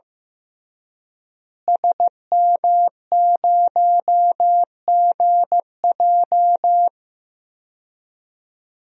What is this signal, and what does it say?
Morse code, heard as a keyed tone, about 15 wpm, 700 Hz. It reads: SM0GJ
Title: DXCC-100 Call Signs - 5 Characters - Repeated 3x - 15wpm